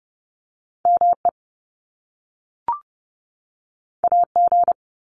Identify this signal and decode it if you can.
Morse code, heard as a keyed tone, about 30 wpm, 700 Hz. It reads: ME AG
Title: Top Two Letter Combinations in Words 30wpm